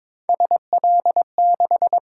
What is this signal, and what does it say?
Morse code, heard as a keyed tone, about 22 wpm, 700 Hz. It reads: SL6
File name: practice / 22wpm / Three Letter-Number Combinations - 22wpm